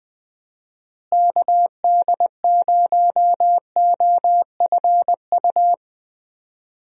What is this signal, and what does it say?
Morse code, heard as a keyed tone, about 21 wpm, 700 Hz. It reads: KD0OFU
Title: USA Call Signs - 6 Characters - Repeated 2x - 20wpm